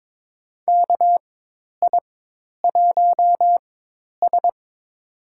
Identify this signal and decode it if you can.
Morse code, heard as a keyed tone, about 23 wpm, spaced as for 10 wpm, 700 Hz. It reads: KI1S
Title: USA Call Signs - 4 Characters - 4x Character Spacing - 22wpm